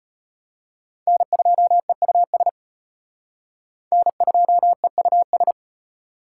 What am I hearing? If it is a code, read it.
Morse code, heard as a keyed tone, about 33 wpm, 700 Hz. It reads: N2EUS N2EUS